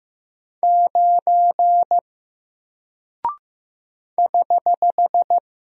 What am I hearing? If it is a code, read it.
Morse code, heard as a keyed tone, about 15 wpm, 700 Hz. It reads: 9 <HH>
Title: Single Character (Letter:Number:Punctuation:Prosign) Alt 15wpm